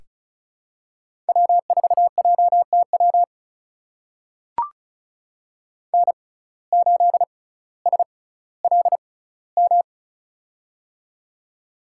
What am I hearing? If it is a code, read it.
Morse code, heard as a keyed tone, about 36 wpm, spaced as for 18 wpm, 700 Hz. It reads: W4JTW N8SLM
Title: USA Call Signs - 5 Characters - 6x Character Spacing - 35wpm